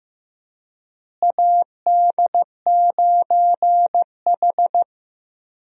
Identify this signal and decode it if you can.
Morse code, heard as a keyed tone, about 15 wpm, 700 Hz. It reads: AD9H
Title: USA Call Signs - 4 Characters - Repeated 3x - 15wpm